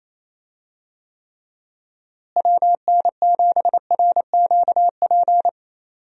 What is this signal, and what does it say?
Morse code, heard as a keyed tone, about 27 wpm, 700 Hz. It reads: WN7RQP
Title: USA Call Signs - 6 Characters - Repeated 2x - 28wpm